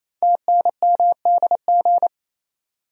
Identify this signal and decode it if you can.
Morse code, heard as a keyed tone, about 27 wpm, 700 Hz. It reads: TNMDG
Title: Code Groups - 5 Letters - 28wpm